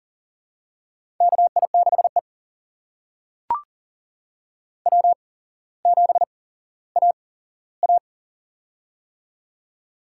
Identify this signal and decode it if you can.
Morse code, heard as a keyed tone, about 40 wpm, spaced as for 17 wpm, 700 Hz. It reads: KI6E W7AA
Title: USA Call Signs - 8x Character Spacing - 40wpm